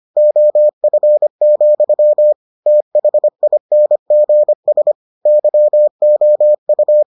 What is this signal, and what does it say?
Morse code, heard as a keyed tone, about 25 wpm, 600 Hz. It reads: OF, THINGS YOU